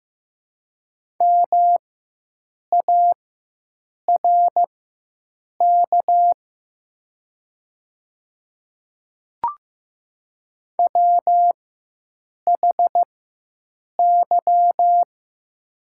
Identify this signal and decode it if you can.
Morse code, heard as a keyed tone, about 15 wpm, spaced as for 7 wpm, 700 Hz. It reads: MARK WHY
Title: 4x Character Spacing - Top 300 Words - 15wpm